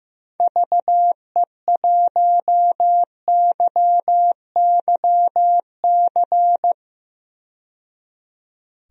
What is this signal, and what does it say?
Morse code, heard as a keyed tone, about 15 wpm, 700 Hz. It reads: VE1YYC
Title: DXCC-100 Call Signs - 6 Characters - 15wpm